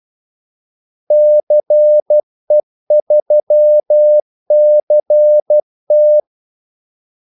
Morse code, heard as a keyed tone, about 12 wpm, 600 Hz. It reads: CE3CT
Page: 2 ekstra ordmellemrum mellem kaldesignalerne